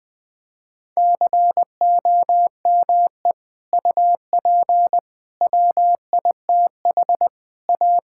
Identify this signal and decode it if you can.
Morse code, heard as a keyed tone, about 20 wpm, 700 Hz. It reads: COME UP WITH A
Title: Sentences with 3-Word Phrases - 20wpm